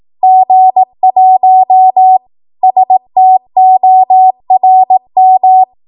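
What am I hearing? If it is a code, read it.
Morse code, heard as a keyed tone, about 18 wpm, 750 Hz. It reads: G1 STORM
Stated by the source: 18 WPM Morse code practice from a national amateur radio association's code archive.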